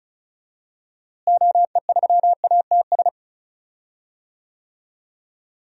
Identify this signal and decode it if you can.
Morse code, heard as a keyed tone, about 34 wpm, 700 Hz. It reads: OE3ATS
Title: DXCC-100 Call Signs - 6 Characters - Repeated 3x - 35wpm